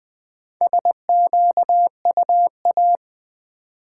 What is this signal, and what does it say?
Morse code, heard as a keyed tone, about 21 wpm, 700 Hz. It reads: SQUA
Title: Build A Word - 1 Letter At A Time - 3 to 5-Letter Words - 20wpm